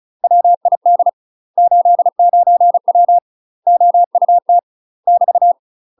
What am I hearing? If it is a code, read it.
Morse code, heard as a keyed tone, about 36 wpm, 700 Hz. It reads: WID 89W OUT <BT>
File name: UK MASTER QSOs TRAINING / 35WPM / UK MASTER QSOs 0017